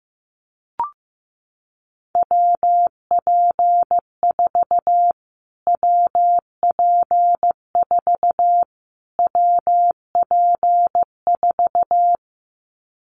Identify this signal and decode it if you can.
Morse code, heard as a keyed tone, about 15 wpm, 700 Hz. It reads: WP4 WP4 WP4
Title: Callsign Prefixes for Top 25 DXCC - Repeated 3x - 15wpm